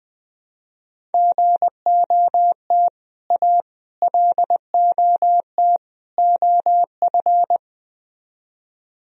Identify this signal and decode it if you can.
Morse code, heard as a keyed tone, about 21 wpm, 700 Hz. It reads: GOT A LOT OF
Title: Speed-Racing - Sets of 4 Words 20wpm